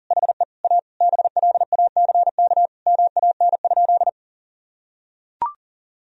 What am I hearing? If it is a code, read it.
Morse code, heard as a keyed tone, about 40 wpm, 700 Hz. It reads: HE A BLACK MAN?
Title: Sentences from Top 1000 Words - Review Part 2 - 40wpm